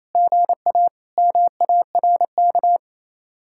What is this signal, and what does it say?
Morse code, heard as a keyed tone, about 33 wpm, 700 Hz. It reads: GA MARK
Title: Speed-Racing Common QSO Phrases - 28wpm